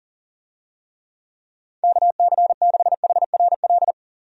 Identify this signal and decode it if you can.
Morse code, heard as a keyed tone, about 40 wpm, 690 Hz. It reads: KC6HRL
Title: USA Call Signs - 6 Characters - 40wpm